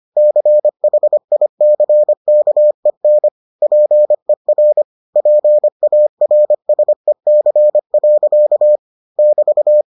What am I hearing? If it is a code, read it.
Morse code, heard as a keyed tone, about 25 wpm, 600 Hz. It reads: CHICKEN PER PARSEC. <BT>